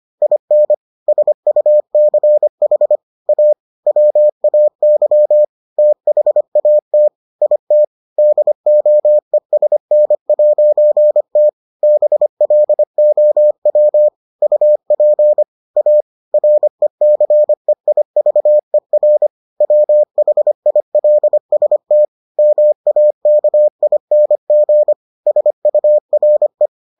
Morse code, heard as a keyed tone, about 25 wpm, 600 Hz. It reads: IN SUCH A WAY THAT IT DOESN'T BLOW UP A RECEIVER WHILST MAKING SURE